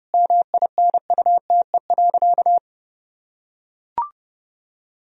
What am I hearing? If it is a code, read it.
Morse code, heard as a keyed tone, about 30 wpm, 700 Hz. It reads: MINUTE.
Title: Sentences from Top 1000 Words - Review Part 2 - 30wpm